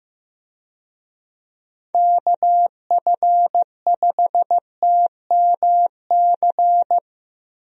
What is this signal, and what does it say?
Morse code, heard as a keyed tone, about 15 wpm, 700 Hz. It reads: KF5TMC
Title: USA Call Signs - 6 Characters - Repeated 3x - 15wpm